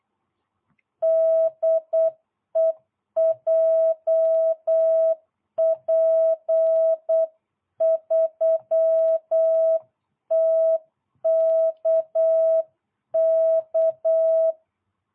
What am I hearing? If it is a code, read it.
Morse code, heard as a keyed tone, about 8 wpm, 660 Hz. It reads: DE JP3TKK